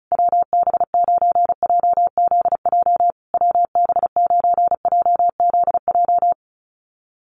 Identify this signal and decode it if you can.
Morse code, heard as a keyed tone, about 34 wpm, 700 Hz. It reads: WB9JZJ WB9JZJ
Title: USA Call Signs Repeated 3x 35wpm